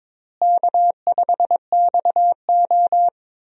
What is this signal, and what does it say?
Morse code, heard as a keyed tone, about 22 wpm, 700 Hz. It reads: K5XO
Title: USA Call Signs - 4 Characters - Repeated 2x - 22wpm